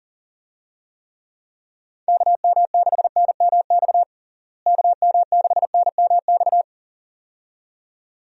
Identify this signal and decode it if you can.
Morse code, heard as a keyed tone, about 40 wpm, 700 Hz. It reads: KM6NMX KM6NMX
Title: USA Call Signs - 6 Characters - Repeated 2x - 40wpm